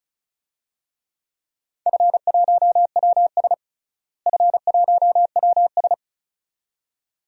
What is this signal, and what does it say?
Morse code, heard as a keyed tone, about 36 wpm, 700 Hz. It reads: F1WS F1WS
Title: DXCC-100 Call Signs - 4 Characters - Repeated 2x - 35wpm